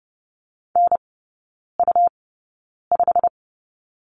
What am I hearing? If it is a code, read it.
Morse code, heard as a keyed tone, about 30 wpm, spaced as for 9 wpm, 700 Hz. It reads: NU5
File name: Lesson 19e - Letter w - Three Characters 30wpm